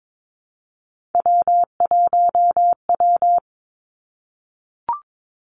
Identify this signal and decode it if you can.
Morse code, heard as a keyed tone, about 22 wpm, 700 Hz. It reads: W1W
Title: Three Challenging Letter-Number Combinations 22wpm